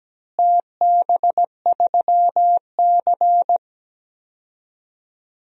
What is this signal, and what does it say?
Morse code, heard as a keyed tone, about 17 wpm, 700 Hz. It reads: TB3C
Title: DXCC-100 Call Signs - 4 Characters - Repeated 2x - 17wpm